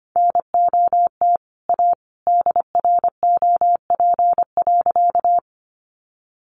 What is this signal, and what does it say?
Morse code, heard as a keyed tone, about 25 wpm, 700 Hz. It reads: NOT A DROP.
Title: Sentences from Top 1400 Words 25wpm